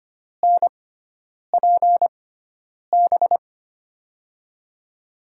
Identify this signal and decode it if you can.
Morse code, heard as a keyed tone, about 25 wpm, spaced as for 9 wpm, 700 Hz. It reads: NPB
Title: USA Call Signs - 6 Characters - 6x Character Spacing - 25wpm